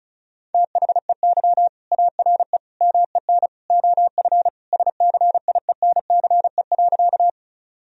Morse code, heard as a keyed tone, about 34 wpm, 700 Hz. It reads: THEY ARE MEN OF SCIENCE.
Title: Sentences from Top 1000 Words - Review Part 2 - 35wpm